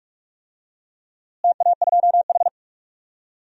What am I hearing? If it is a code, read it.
Morse code, heard as a keyed tone, about 42 wpm, 700 Hz. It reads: TA2H
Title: DXCC-100 Call Signs - 4 Characters - Repeated 2x - 45wpm